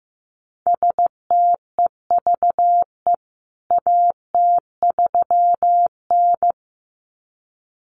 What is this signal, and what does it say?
Morse code, heard as a keyed tone, about 15 wpm, 700 Hz. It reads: STEVE AT3N
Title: CWT Exchange 15wpm